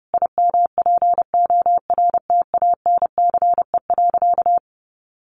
Morse code, heard as a keyed tone, about 30 wpm, 700 Hz. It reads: IMPORTANCE.
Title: Sentences from Top 1200 Words 30wpm